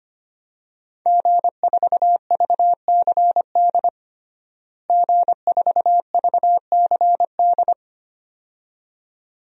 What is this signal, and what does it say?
Morse code, heard as a keyed tone, about 25 wpm, 700 Hz. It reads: G4VCD G4VCD
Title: DXCC-100 Call Signs - 5 Characters - Repeated 2x - 25wpm